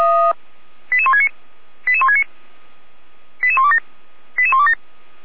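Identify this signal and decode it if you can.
Five-tone selective call (ZVEI), heard as quick runs of five tones, tones of about 70 ms. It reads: 88169 88169 88127 88127
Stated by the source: ZVEI five-tone selective call 88367 received by SDR.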